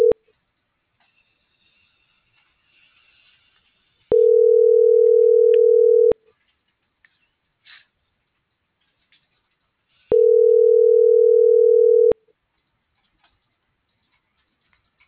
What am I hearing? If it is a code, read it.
Ringback tone (North America), 440 + 480 Hz.
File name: CENTRAL NAVIERAS / out leonardo.pe.a